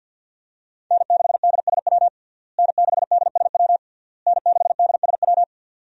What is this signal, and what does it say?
Morse code, heard as a keyed tone, about 52 wpm, 690 Hz. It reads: N6DSW N6DSW N6DSW
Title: USA Call Signs - 5 Characters - Repeated 3x - 50wpm